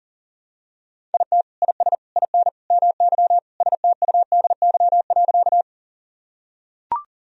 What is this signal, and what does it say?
Morse code, heard as a keyed tone, about 40 wpm, 700 Hz. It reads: IT IS IN MY STUDY.
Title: Sentences from Top 1000 Words - Review Part 1 - 40wpm